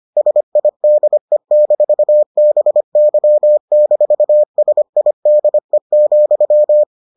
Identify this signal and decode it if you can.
Morse code, heard as a keyed tone, about 25 wpm, 600 Hz. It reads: SIDE-BY-SIDE,